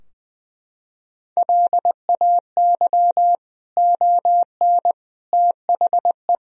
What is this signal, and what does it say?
Morse code, heard as a keyed tone, about 21 wpm, 700 Hz. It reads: LAY ON THE
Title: Sentences from Top 1000 Words - Review Part 2 - 20wpm